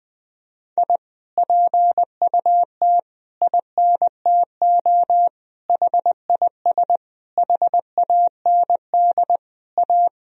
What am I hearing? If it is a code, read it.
Morse code, heard as a keyed tone, about 21 wpm, 700 Hz. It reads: I PUT INTO HIS HAND A
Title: Sentences from Top 1000 Words - Review Part 2 - 20wpm